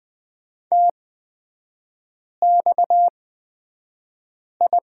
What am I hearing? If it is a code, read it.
Morse code, heard as a keyed tone, about 20 wpm, spaced as for 5 wpm, 690 Hz. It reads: TXI
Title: 229 Lesson 37e - Letter x - Three Characters 20wpm